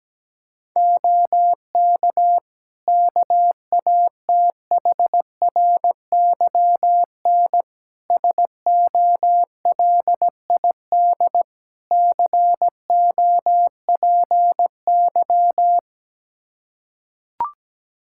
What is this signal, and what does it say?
Morse code, heard as a keyed tone, about 17 wpm, 700 Hz. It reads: OK KATHRYN SOLID COPY
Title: K1USN SST QSO Practice 17wpm